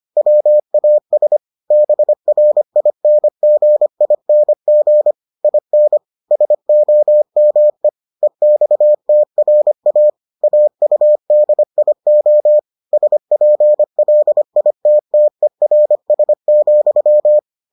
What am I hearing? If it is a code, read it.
Morse code, heard as a keyed tone, about 25 wpm, 600 Hz. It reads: WAS BRINGING IN SOME EXTRA AUDIO SPLITTERS,